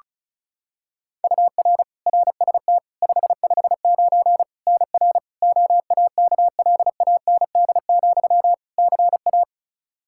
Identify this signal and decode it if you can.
Morse code, heard as a keyed tone, about 34 wpm, 700 Hz. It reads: UR RST 559 NR OAKLAND, CA
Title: K1USN SST QSO Practice 35wpm